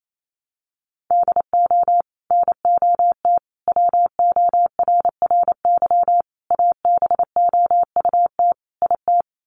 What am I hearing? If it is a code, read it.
Morse code, heard as a keyed tone, about 28 wpm, 700 Hz. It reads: DO NOT WORRY ABOUT IT
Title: Sets of 5 Words 28wpm